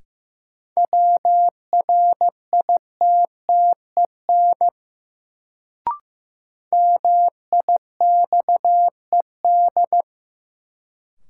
Morse code, heard as a keyed tone, about 15 wpm, 700 Hz. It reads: WRITTEN MIXED
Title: Sentences from Top 1500 Words - New Word Drill - 15wpm